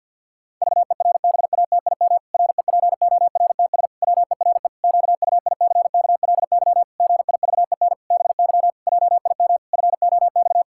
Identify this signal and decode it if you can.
Morse code, heard as a keyed tone, about 51 wpm, 700 Hz. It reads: VERBATIM REPORTS WERE QUICKLY GIVEN BY JIM FOX
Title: Pangram Challenge v2 - 50wpm